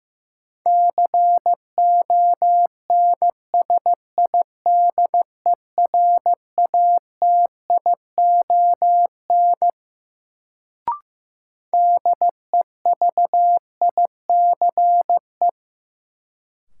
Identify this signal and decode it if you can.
Morse code, heard as a keyed tone, about 15 wpm, 700 Hz. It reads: CONSIDERATION DEVICE
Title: Sentences from Top 1500 Words - New Word Drill - 15wpm